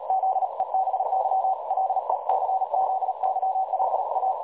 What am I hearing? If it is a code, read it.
Morse code, heard as a keyed tone, about 38 wpm, 760 Hz, with noise. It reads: ZEBY LECZYLA
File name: dlazaawansowanych / nagraniazpasma / websdr recording 3565.0kHz